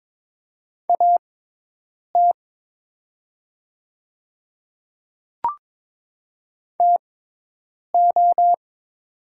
Morse code, heard as a keyed tone, about 22 wpm, spaced as for 8 wpm, 700 Hz. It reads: AT TO
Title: 6x Character Spacing - Top 100 Words - 22wpm